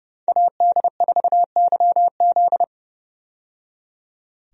Morse code, heard as a keyed tone, about 30 wpm, 700 Hz. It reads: AD4YZ
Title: USA Call Signs - 5 Characters - 30wpm